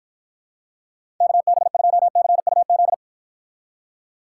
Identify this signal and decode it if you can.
Morse code, heard as a keyed tone, about 53 wpm, 700 Hz. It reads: KB2CUZ